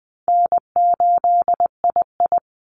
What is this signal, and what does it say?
Morse code, heard as a keyed tone, about 20 wpm, 700 Hz. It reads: N8II
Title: Easy Call Signs Repeated 2x - 20wpm